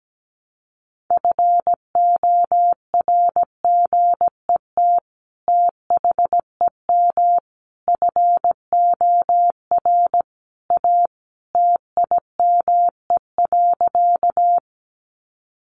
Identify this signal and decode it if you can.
Morse code, heard as a keyed tone, about 17 wpm, 700 Hz. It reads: FORGET THEM FOR A TIME.
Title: Sentences from Top 1200 Words 17wpm